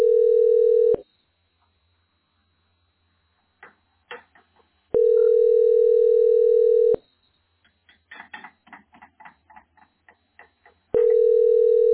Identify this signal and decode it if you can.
Ringback tone (North America), 440 + 480 Hz.